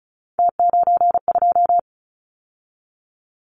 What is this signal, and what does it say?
Morse code, heard as a keyed tone, about 34 wpm, 700 Hz. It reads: T92
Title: Callsign Prefixes for Top 100 DXCC - 35wpm